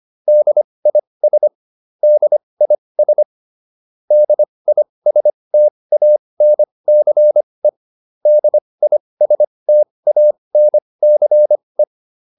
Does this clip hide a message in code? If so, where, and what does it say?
Yes: Morse code from 0.3 to 11.8 s, heard as a keyed tone, about 25 wpm, spaced as for 20 wpm, 600 Hz. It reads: DIS DIS DISTANCE DISTANCE